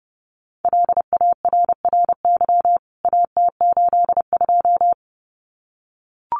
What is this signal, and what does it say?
Morse code, heard as a keyed tone, about 30 wpm, 700 Hz. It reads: LARRY AT82
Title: CWT Exchange 30wpm